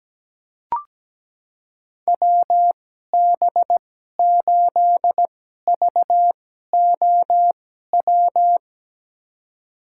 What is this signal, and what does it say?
Morse code, heard as a keyed tone, about 17 wpm, spaced as for 12 wpm, 700 Hz. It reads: WB8VOW
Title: USA Call Signs - 6 Characters - 2x Character Spacing - 17wpm